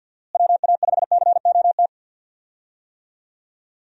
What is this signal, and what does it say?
Morse code, heard as a keyed tone, about 46 wpm, 700 Hz. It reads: WA5COT
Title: USA Call Signs - 6 Characters - Repeated 3x - 50wpm